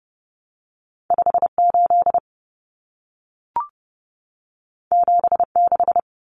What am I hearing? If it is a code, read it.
Morse code, heard as a keyed tone, about 30 wpm, 700 Hz. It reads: 58 76
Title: Numbers - 2-Digit 30wpm